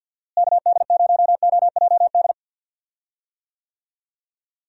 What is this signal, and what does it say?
Morse code, heard as a keyed tone, about 52 wpm, 700 Hz. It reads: KD0OJD
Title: USA Call Signs - 6 Characters - Repeated 2x - 50wpm